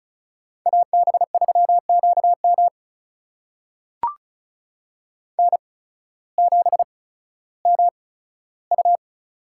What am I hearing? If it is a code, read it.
Morse code, heard as a keyed tone, about 36 wpm, spaced as for 16 wpm, 700 Hz. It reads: AB3QM N7MU
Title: USA Call Signs - 8x Character Spacing - 35wpm